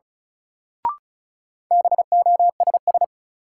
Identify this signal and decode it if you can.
Morse code, heard as a keyed tone, about 36 wpm, 700 Hz. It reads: BOSS
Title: Sentences from Top 700 Words - New Word Drill - 35wpm